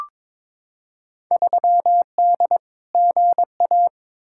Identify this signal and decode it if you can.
Morse code, heard as a keyed tone, about 22 wpm, 700 Hz. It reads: 3D GA
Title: Field Day Exchanges Encore - 22wpm